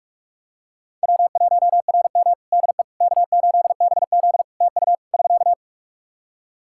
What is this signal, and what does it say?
Morse code, heard as a keyed tone, about 44 wpm, 700 Hz. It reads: W1RM DE K8BZ TU <SK>
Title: Practice QSO - Exchange 4 - Ending - 45wpm